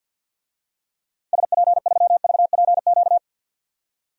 Morse code, heard as a keyed tone, about 52 wpm, 700 Hz. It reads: SP3VPQ
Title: DXCC-100 Call Signs - 6 Characters - 50wpm